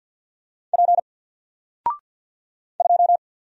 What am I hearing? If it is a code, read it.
Morse code, heard as a keyed tone, about 50 wpm, 700 Hz. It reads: P 2
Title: Single Character (Letter:Number:Punctuation:Prosign) Alt 50wpm